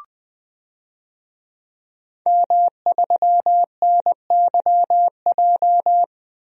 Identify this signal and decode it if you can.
Morse code, heard as a keyed tone, about 20 wpm, 700 Hz. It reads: M3NYJ
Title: DXCC-100 Call Signs - 5 Characters - Repeated 2x - 20wpm